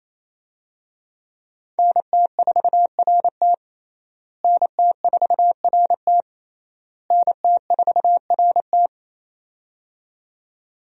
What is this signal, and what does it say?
Morse code, heard as a keyed tone, about 28 wpm, 700 Hz. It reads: NT4RT NT4RT NT4RT
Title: USA Call Signs - 5 Characters - Repeated 3x - 28wpm